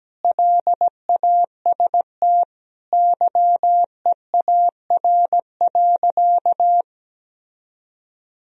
Morse code, heard as a keyed tone, about 17 wpm, 700 Hz. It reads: LAST YEAR.
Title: Sentences from Top 1000 Words - Review Part 1 - 17wpm